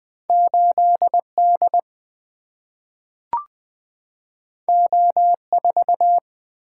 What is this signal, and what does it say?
Morse code, heard as a keyed tone, about 20 wpm, 700 Hz. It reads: 8D O4
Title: Single-Letter Single-Number 20wpm